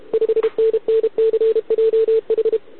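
Morse code, heard as a keyed tone, about 32 wpm, 440 Hz. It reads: 5NNCJH
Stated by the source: Contest CW from OK1RI on 80 m during the OKOM DX contest, recorded by a QRP station using a KX3 and a wire antenna.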